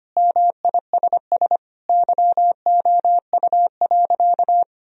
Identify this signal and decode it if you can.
Morse code, heard as a keyed tone, about 25 wpm, 700 Hz. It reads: MISS YOU.
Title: Sentences from Top 1000 Words - Review Part 3 - 25wpm